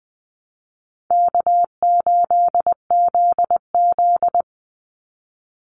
Morse code, heard as a keyed tone, about 20 wpm, 700 Hz. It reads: K8ZZ
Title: Easy Call Signs - 20wpm